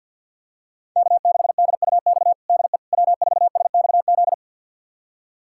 Morse code, heard as a keyed tone, about 52 wpm, 700 Hz. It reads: K6DUX DE W4SXZ
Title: USA Callsign DE Callsign - 5 Characters - 50wpm